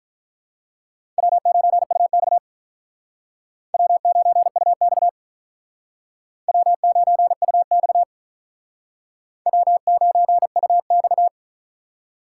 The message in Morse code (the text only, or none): W9UX W9UX W9UX W9UX